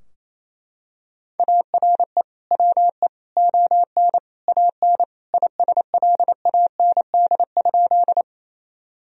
Morse code, heard as a keyed tone, about 28 wpm, 700 Hz. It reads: ARE WE ON AN ISLAND?
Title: Sentences from Top 1000 Words - Review Part 3 - 28wpm